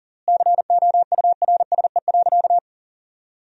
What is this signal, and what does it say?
Morse code, heard as a keyed tone, about 40 wpm, 700 Hz. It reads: COURSE.
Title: 3x Word-Spacing - Sentences from Top 500 Words - 40wpm